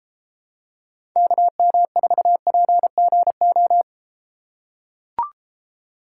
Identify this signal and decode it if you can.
Morse code, heard as a keyed tone, about 32 wpm, 700 Hz. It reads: KM4PGO